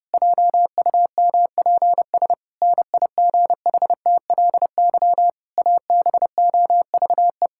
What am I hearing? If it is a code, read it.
Morse code, heard as a keyed tone, about 32 wpm, 700 Hz. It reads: JUMPS NIGHTLY ABOVE
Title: Pangram Challenge v2 - 30wpm